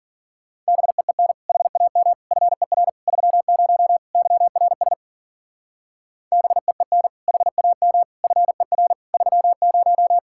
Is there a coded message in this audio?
Yes: Morse code from 0.7 to 10.2 s, heard as a keyed tone, about 52 wpm, 700 Hz. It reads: BEEN HAM FER 30 YRS BEEN HAM FER 30